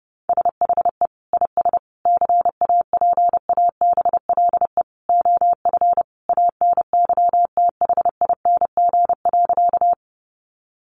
Morse code, heard as a keyed tone, about 30 wpm, 700 Hz. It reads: SHE IS CAPABLE OF ANYTHING.
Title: Sentences from Top 1200 Words 30wpm